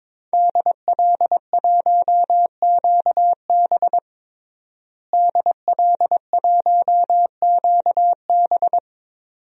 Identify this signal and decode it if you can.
Morse code, heard as a keyed tone, about 22 wpm, 700 Hz. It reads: DL1QB DL1QB
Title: DXCC-100 Call Signs - 5 Characters - Repeated 2x - 22wpm